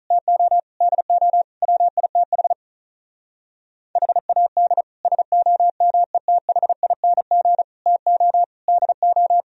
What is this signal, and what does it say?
Morse code, heard as a keyed tone, about 34 wpm, 700 Hz. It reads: TO DO WITH HAD SOMETHING TO DO
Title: Speed-Racing - Sets of 5 Words 35wpm